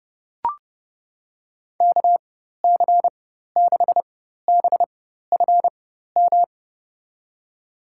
Morse code, heard as a keyed tone, about 30 wpm, spaced as for 14 wpm, 700 Hz. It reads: KC6BFM
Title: USA Call Signs - 6 Characters - 4x Character Spacing - 30wpm